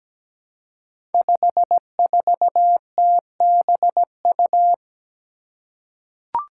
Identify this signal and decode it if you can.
Morse code, heard as a keyed tone, about 17 wpm, 700 Hz. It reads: 54TBU
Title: Five Challenging Letter-Number Combinations 17wpm